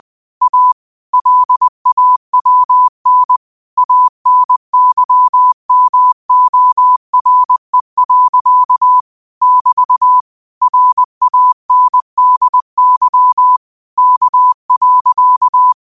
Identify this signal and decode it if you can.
Morse code, heard as a keyed tone, about 20 wpm, 1000 Hz. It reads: A LAWN ANYMORE. <BT> RANDY K.